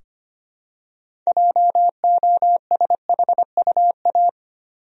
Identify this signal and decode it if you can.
Morse code, heard as a keyed tone, about 25 wpm, 700 Hz. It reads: JOSHUA
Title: USA Common Names - 25wpm